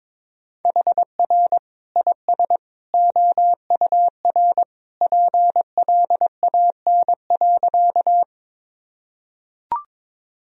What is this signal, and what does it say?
Morse code, heard as a keyed tone, about 22 wpm, 700 Hz. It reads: HR IS OUR PLAN.
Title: Sentences from Top 500 Words and Abbreviations 22wpm